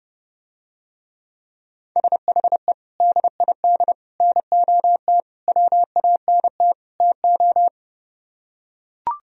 Morse code, heard as a keyed tone, about 30 wpm, 700 Hz. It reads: SHE DID NOT WANT TO